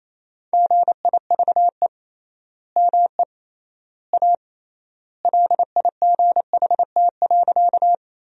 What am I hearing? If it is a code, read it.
Morse code, heard as a keyed tone, about 28 wpm, 700 Hz. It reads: GIVE ME A LIGHT.
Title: 3x Word-Spacing - Sentences from Top 200 Words - 28wpm